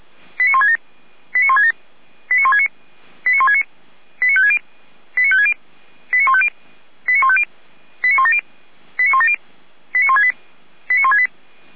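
Five-tone selective call (ZVEI), heard as quick runs of five tones, tones of about 80 ms. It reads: 89167 89167 89169 89169 89560 89560 89150 89150 89170 89170 89168 89168
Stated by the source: Five-tone ZVEI selective call received by SDR, from the Feuerwehr Orlamünde (fire brigade) audio server.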